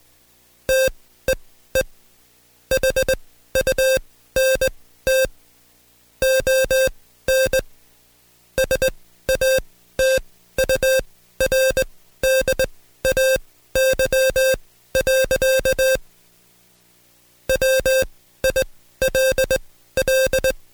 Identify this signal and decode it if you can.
Morse code, heard as a keyed tone, about 20 wpm, spaced as for 13 wpm, 530 Hz. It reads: TEE HUNT ON SATURDAY. WILL
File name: Code Quick / AudioDownld2 / Seg133-Stepper6 13WPM